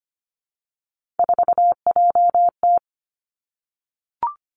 Morse code, heard as a keyed tone, about 25 wpm, 700 Hz. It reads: 4JT
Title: Three Challenging Letter-Number Combinations 25wpm